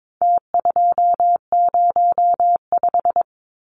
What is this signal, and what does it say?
Morse code, heard as a keyed tone, about 22 wpm, 700 Hz. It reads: T205
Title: Numbers - Contest Style - 4-Digit 22wpm